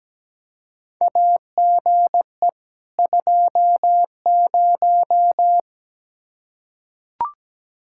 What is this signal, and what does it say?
Morse code, heard as a keyed tone, about 17 wpm, 700 Hz. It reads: AGE 20
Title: Practice QSO - Exchange 3 - More Info - 17wpm